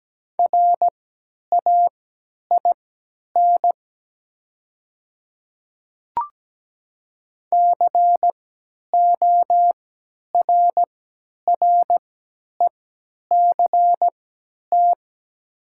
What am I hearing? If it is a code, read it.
Morse code, heard as a keyed tone, about 17 wpm, spaced as for 10 wpm, 700 Hz. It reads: RAIN CORRECT
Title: 3x Character Spacing - Top 500 Words - 17wpm